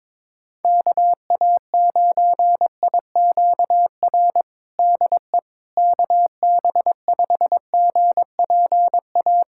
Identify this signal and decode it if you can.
Morse code, heard as a keyed tone, about 22 wpm, 700 Hz. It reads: KA9IQR DE KB5GPA
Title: USA Callsign DE Callsign - 6 Characters - 22wpm